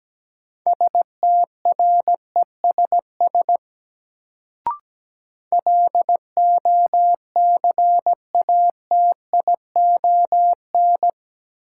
Morse code, heard as a keyed tone, about 17 wpm, 700 Hz. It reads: STRESS LOCATION
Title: Sentences from Top 1000 Words - Review Part 6 - New Word Drill - 17wpm